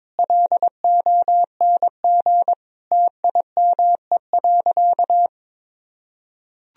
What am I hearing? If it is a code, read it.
Morse code, heard as a keyed tone, about 22 wpm, 700 Hz. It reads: LONG TIME.
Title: Sentences from Top 1000 Words - Review Part 1 - 22wpm